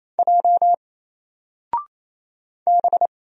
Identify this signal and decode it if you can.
Morse code, heard as a keyed tone, about 28 wpm, 700 Hz. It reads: J B
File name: Single-Letter-Number-28wpm